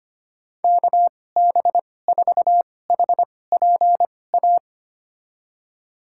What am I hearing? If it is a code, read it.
Morse code, heard as a keyed tone, about 25 wpm, spaced as for 18 wpm, 700 Hz. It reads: KB4HPA
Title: USA Call Signs - 6 Characters - 2x Character Spacing - 25wpm